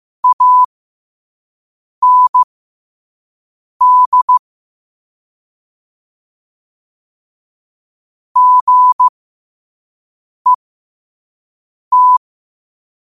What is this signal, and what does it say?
Morse code, heard as a keyed tone, about 15 wpm, spaced as for 5 wpm, 1000 Hz. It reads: AND GET
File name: qotdfiles / 5wpm